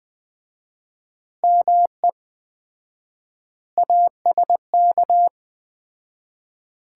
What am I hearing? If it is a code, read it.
Morse code, heard as a keyed tone, about 20 wpm, 700 Hz. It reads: ME ASK
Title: 4x Word-Spacing - Sets of 5 Words - 20wpm